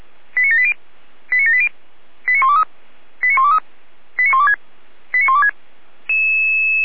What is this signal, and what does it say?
Five-tone selective call (ZVEI), heard as quick runs of five tones, tones of about 70 ms. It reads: 89780 89780 89123 89123 89126 89126
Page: audio.f85360.feuerwehr-orlamuende.org